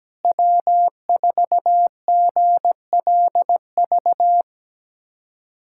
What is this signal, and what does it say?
Morse code, heard as a keyed tone, about 17 wpm, 700 Hz. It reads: W4GLV
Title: USA Callsign DE Callsign - 5 Characters - 17wpm